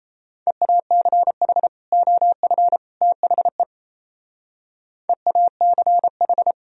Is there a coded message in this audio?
Yes: Morse code from 0.5 to 6.5 s, heard as a keyed tone, about 36 wpm, 700 Hz. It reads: EACH OF THE EACH